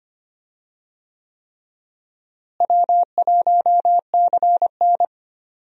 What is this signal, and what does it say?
Morse code, heard as a keyed tone, about 25 wpm, 700 Hz. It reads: W1CN
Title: USA Call Signs - 4 Characters - Repeated 3x - 25wpm